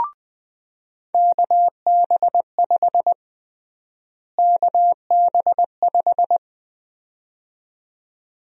Morse code, heard as a keyed tone, about 20 wpm, 700 Hz. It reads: KB5 KB5
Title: USA Call Sign Prefixes - Repeated 2x - 20wpm